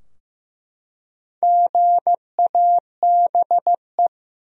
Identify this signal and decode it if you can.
Morse code, heard as a keyed tone, about 15 wpm, 700 Hz. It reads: GABE
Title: USA Common Names - Max 4 Letters - 15wpm